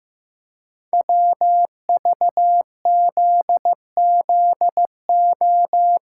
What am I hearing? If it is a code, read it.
Morse code, heard as a keyed tone, about 15 wpm, 700 Hz. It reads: WVZZO
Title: Code Groups - 5 Letters - 15wpm